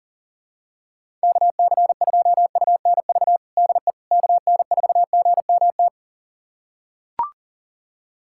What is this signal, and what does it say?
Morse code, heard as a keyed tone, about 40 wpm, 700 Hz. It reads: KC2UNV DE KN4GMT
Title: USA Callsign DE Callsign - 6 Characters - 40wpm